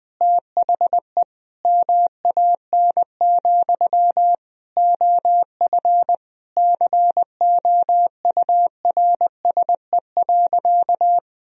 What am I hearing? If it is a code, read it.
Morse code, heard as a keyed tone, about 21 wpm, 700 Hz. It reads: THE MAN, OF COURSE.
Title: Sentences from Top 1000 Words - Review Part 1 - 20wpm